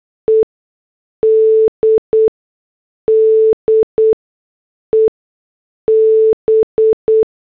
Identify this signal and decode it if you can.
Morse code, heard as a keyed tone, about 8 wpm, spaced as for 6 wpm, 440 Hz. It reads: ED DEB
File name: morse / headlines